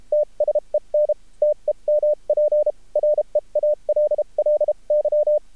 Morse code, heard as a keyed tone, about 32 wpm, spaced as for 28 wpm, 600 Hz. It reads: TSEN TEMP REALLY
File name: archived / CRSnet